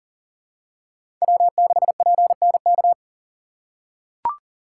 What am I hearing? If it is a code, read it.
Morse code, heard as a keyed tone, about 40 wpm, 700 Hz. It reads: W6PNK
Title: USA Call Signs - 5 Characters - 4x Character Spacing - 40wpm